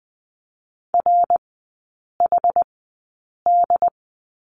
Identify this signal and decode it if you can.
Morse code, heard as a keyed tone, about 20 wpm, spaced as for 8 wpm, 700 Hz. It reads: RHD
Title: 138 Lesson 23e - Letter f - Three Characters 20wpm